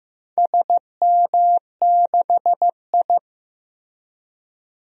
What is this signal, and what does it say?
Morse code, heard as a keyed tone, about 15 wpm, 700 Hz. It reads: SM6I
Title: DXCC-100 Call Signs - 4 Characters - 15wpm